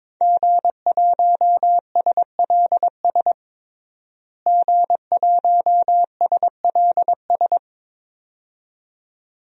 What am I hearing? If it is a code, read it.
Morse code, heard as a keyed tone, about 22 wpm, 700 Hz. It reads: G1SLS G1SLS
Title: DXCC-100 Call Signs - 5 Characters - Repeated 2x - 22wpm